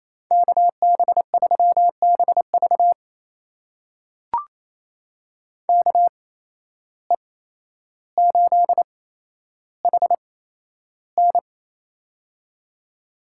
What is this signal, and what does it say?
Morse code, heard as a keyed tone, about 28 wpm, spaced as for 12 wpm, 700 Hz. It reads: KB3BV KE8HN
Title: USA Call Signs - 5 Characters - 8x Character Spacing - 28wpm